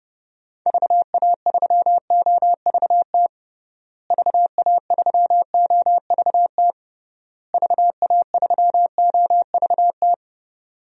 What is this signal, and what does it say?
Morse code, heard as a keyed tone, about 29 wpm, 700 Hz. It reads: VA3OVT VA3OVT VA3OVT
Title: DXCC-100 Call Signs - 6 Characters - Repeated 3x - 30wpm